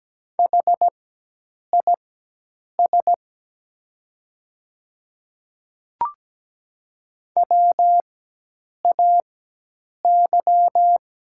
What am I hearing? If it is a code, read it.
Morse code, heard as a keyed tone, about 17 wpm, spaced as for 8 wpm, 700 Hz. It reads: HIS WAY
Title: 4x Character Spacing - Top 100 Words - 17wpm